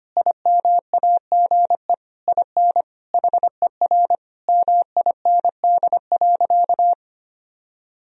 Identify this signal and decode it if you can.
Morse code, heard as a keyed tone, about 25 wpm, 700 Hz. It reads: IMAGE IN HER MIND.
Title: Sentences from Top 1000 Words - Review Part 4 - 25wpm